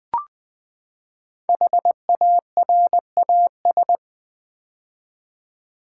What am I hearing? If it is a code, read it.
Morse code, heard as a keyed tone, about 20 wpm, 700 Hz. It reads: HARAS
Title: Build A Word - 5 Letters At A Time - 10-Letter Words - 20wpm